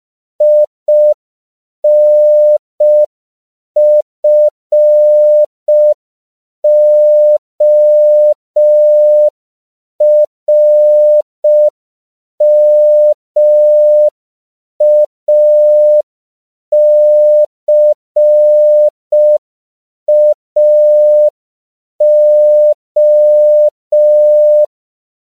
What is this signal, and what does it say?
Morse code, heard as a keyed tone, about 5 wpm, 600 Hz. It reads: INFORMACAO
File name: CW / 5PPM